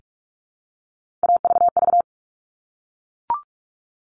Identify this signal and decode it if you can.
Morse code, heard as a keyed tone, about 44 wpm, 700 Hz. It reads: AVV
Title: All Cut Numbers - 3-Digit 45wpm